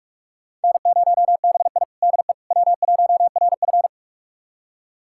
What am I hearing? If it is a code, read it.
Morse code, heard as a keyed tone, about 41 wpm, 700 Hz. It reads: N0BI DE W1RF
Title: USA Callsign DE Callsign - 4 Characters - 45wpm